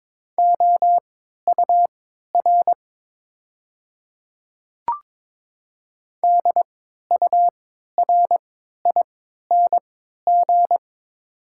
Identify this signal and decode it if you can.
Morse code, heard as a keyed tone, about 22 wpm, spaced as for 12 wpm, 700 Hz. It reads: OUR DURING
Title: 3x Character Spacing - Top 500 Words - 22wpm